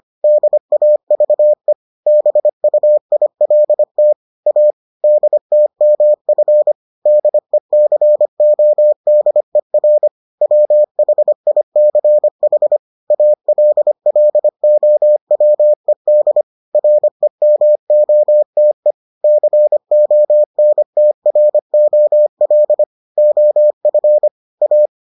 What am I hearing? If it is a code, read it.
Morse code, heard as a keyed tone, about 25 wpm, 600 Hz. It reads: DAVE BUILT A DTMF DECODER WHICH ALLOWED REMOTE CONTROL OF A